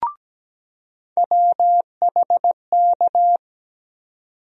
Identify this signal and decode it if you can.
Morse code, heard as a keyed tone, about 17 wpm, 700 Hz. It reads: WHK
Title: USA Call Sign Postfixes - 3 Letters 17wpm